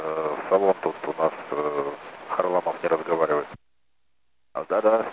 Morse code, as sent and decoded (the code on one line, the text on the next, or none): none